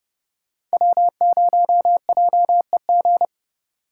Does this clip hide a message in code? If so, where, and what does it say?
Yes: Morse code from 0.7 to 3.2 s, heard as a keyed tone, about 34 wpm, 700 Hz. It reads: W0JEG